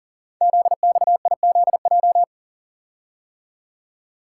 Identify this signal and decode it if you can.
Morse code, heard as a keyed tone, about 39 wpm, 700 Hz. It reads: ZXIZJ
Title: Code Groups - 5 Letters - 40wpm